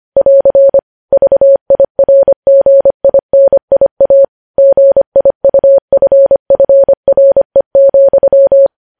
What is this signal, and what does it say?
Morse code, heard as a keyed tone, about 25 wpm, 550 Hz. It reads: <AR> VIRGINIA GIUFFRE,
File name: podcast / News headlines-1x-25x25